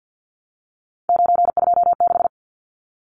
Morse code, heard as a keyed tone, about 50 wpm, 700 Hz. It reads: 926
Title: Numbers - 3-Digit 50wpm